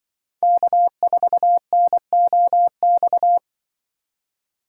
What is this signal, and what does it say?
Morse code, heard as a keyed tone, about 24 wpm, 700 Hz. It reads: K4NOX
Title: Speed Racing - USA Call Signs 20wpm